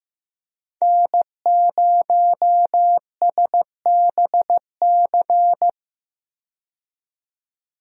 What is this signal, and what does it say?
Morse code, heard as a keyed tone, about 15 wpm, 700 Hz. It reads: N0SBC
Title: USA Call Signs - 5 Characters - 15wpm